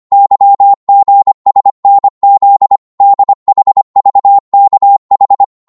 Morse code, heard as a keyed tone, about 25 wpm, 800 Hz. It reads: YGSNZ DHVKH